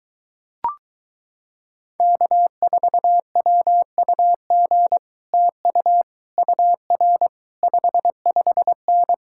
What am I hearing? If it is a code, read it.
Morse code, heard as a keyed tone, about 24 wpm, 700 Hz. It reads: K4WUG TU UR 55N